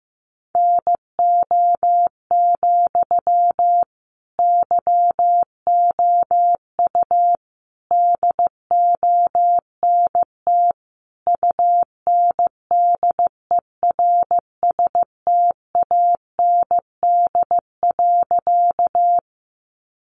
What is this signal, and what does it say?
Morse code, heard as a keyed tone, about 15 wpm, 700 Hz. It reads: NO, YOU DONT UNDERSTAND.
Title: Sentences from Top 600 Words 15wpm